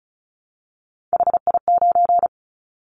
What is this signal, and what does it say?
Morse code, heard as a keyed tone, about 36 wpm, 700 Hz. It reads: HI9
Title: Callsign Prefixes for Top 100 DXCC - 35wpm